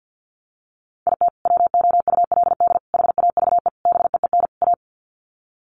Morse code, heard as a keyed tone, about 50 wpm, 700 Hz. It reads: IT WOULD HAVE BEEN A
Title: Sets of 5 Words Encore 50wpm